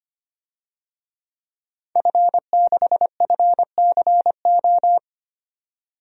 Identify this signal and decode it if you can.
Morse code, heard as a keyed tone, about 25 wpm, 700 Hz. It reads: F6FCO
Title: DXCC-100 Call Signs - 5 Characters - 25wpm